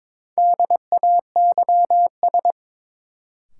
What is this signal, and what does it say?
Morse code, heard as a keyed tone, about 22 wpm, 700 Hz. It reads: DAYS
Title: Sets of 5 Words Encore - New Word Drill - 22wpm